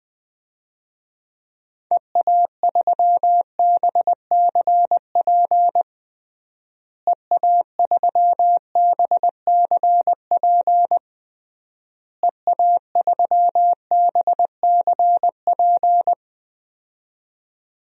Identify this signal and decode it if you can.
Morse code, heard as a keyed tone, about 20 wpm, 700 Hz. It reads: EA3BCP EA3BCP EA3BCP
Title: DXCC-100 Call Signs - 6 Characters - Repeated 3x - 20wpm